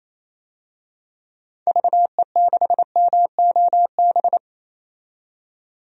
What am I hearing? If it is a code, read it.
Morse code, heard as a keyed tone, about 28 wpm, 700 Hz. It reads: VE6MOB
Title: DXCC-100 Call Signs - 6 Characters - 28wpm